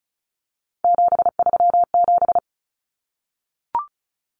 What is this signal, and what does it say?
Morse code, heard as a keyed tone, about 34 wpm, 700 Hz. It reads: 737
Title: Numbers - 3-Digit 35wpm